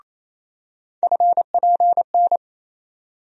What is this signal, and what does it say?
Morse code, heard as a keyed tone, about 28 wpm, 700 Hz. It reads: FPN
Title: USA Call Sign Postfixes - 3 Letters 28wpm